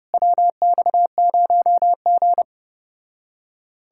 Morse code, heard as a keyed tone, about 33 wpm, 700 Hz. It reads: WX0G